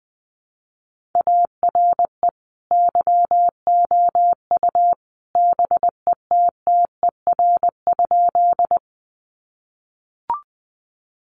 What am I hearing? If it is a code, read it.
Morse code, heard as a keyed tone, about 20 wpm, 700 Hz. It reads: ARE YOU BETTER?
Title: Sentences from Top 400 Words 20wpm